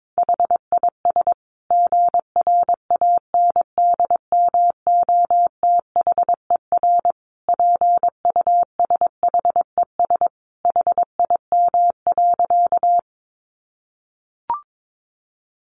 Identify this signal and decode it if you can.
Morse code, heard as a keyed tone, about 22 wpm, 700 Hz. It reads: HIS GRANDMOTHER PUSHES HIM.
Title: Additional Easy Phrases 22wpm